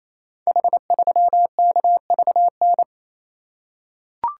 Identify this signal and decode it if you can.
Morse code, heard as a keyed tone, about 27 wpm, 700 Hz. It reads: H3KVN
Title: Code Groups - 5 Letter:Numbers - 28wpm